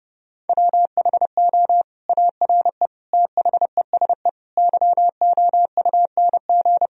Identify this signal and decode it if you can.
Morse code, heard as a keyed tone, about 30 wpm, 700 Hz. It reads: WHO ARE THESE YOUNG
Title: Sentences from Top 1000 Words - Review Part 1 - 30wpm